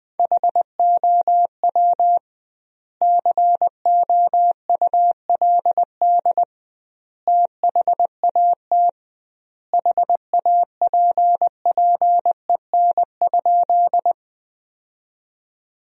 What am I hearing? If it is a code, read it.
Morse code, heard as a keyed tone, about 20 wpm, 700 Hz. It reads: HOW COULD THAT HAPPEN?
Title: 2x Word-Spacing - Sentences from Top 400 Words - 20wpm